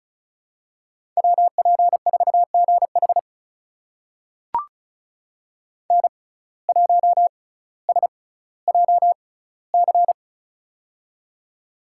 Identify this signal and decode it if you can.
Morse code, heard as a keyed tone, about 36 wpm, spaced as for 18 wpm, 700 Hz. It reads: WP4GH N1SJC
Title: USA Call Signs - 5 Characters - 6x Character Spacing - 35wpm